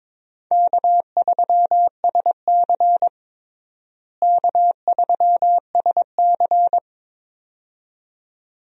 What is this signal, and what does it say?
Morse code, heard as a keyed tone, about 22 wpm, 700 Hz. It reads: K3SC K3SC
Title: USA Call Signs - 4 Characters - Repeated 2x - 22wpm